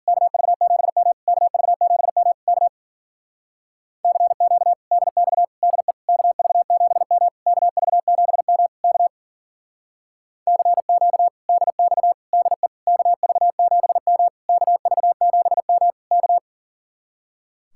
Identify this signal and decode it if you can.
Morse code, heard as a keyed tone, about 48 wpm, 700 Hz. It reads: KV7M KV7M K CQ DX DE KV7M KV7M K CQ DX DE KV7M KV7M K
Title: Speed-Racing Common QSO Phrases - 40wpm